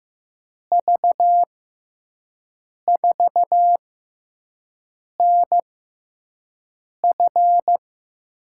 Morse code, heard as a keyed tone, about 15 wpm, spaced as for 5 wpm, 700 Hz. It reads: V4NF
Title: USA Call Signs - 6x Character Spacing - 15wpm